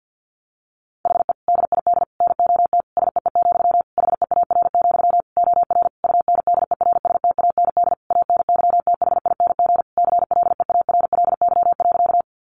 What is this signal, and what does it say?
Morse code, heard as a keyed tone, about 48 wpm, 700 Hz. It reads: HE DID NOT SEE, HEAR, OR UNDERSTAND ANYTHING CLEARLY.
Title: Sentences from Top 1300 Words 50wpm